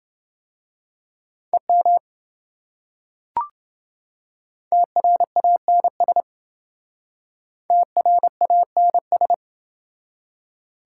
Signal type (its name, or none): Morse code